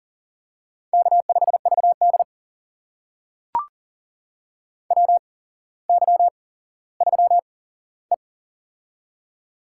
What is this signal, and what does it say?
Morse code, heard as a keyed tone, about 42 wpm, spaced as for 17 wpm, 700 Hz. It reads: K5VD WY3E
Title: USA Call Signs - 4 Characters - 8x Character Spacing - 40wpm